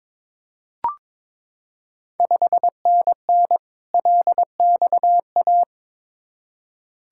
Morse code, heard as a keyed tone, about 22 wpm, 700 Hz. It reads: 5NN LXA
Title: Ontario QSO Party Exchange 22wpm